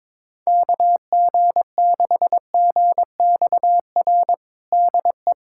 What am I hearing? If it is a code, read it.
Morse code, heard as a keyed tone, about 22 wpm, 700 Hz. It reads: KG6GXR DE
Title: USA Callsign DE Callsign 22wpm